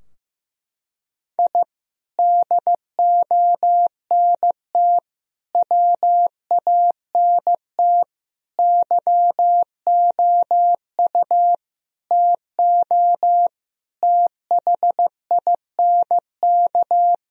Morse code, heard as a keyed tone, about 15 wpm, 700 Hz. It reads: I DONT WANT YOU TO THINK
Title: Sentences from Top 1000 Words - Review Part 3 - 15wpm